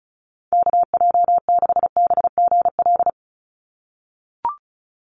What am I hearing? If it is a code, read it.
Morse code, heard as a keyed tone, about 36 wpm, 700 Hz. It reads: KJ6BGL
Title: USA Call Signs 35wpm